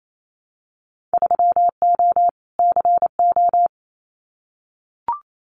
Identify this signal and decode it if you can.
Morse code, heard as a keyed tone, about 28 wpm, 690 Hz. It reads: 3O CO
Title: Winter Field Day Exchanges Encore 28wpm